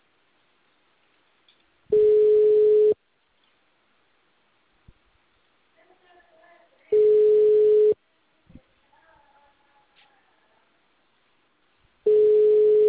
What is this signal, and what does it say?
Ringback tone (Europe, 425 Hz), with speech.